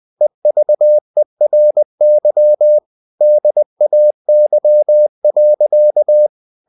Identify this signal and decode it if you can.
Morse code, heard as a keyed tone, about 20 wpm, 600 Hz. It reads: EVERY DAY.
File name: the old man and the sea / the old man and the sea part 1 20wpm